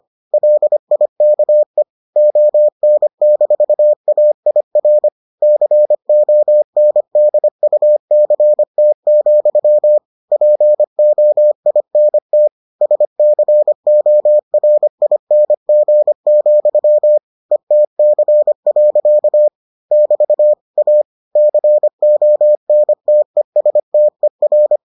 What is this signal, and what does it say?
Morse code, heard as a keyed tone, about 25 wpm, 600 Hz. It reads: LIKE ON-AIR CONDUCT, POINT SCORING, ETC. <BT> A CONTESTER